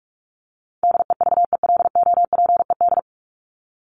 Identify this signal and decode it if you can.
Morse code, heard as a keyed tone, about 44 wpm, 700 Hz. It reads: DEVELOPED
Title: ED Suffix - Top 100 Words with Base Words 45wpm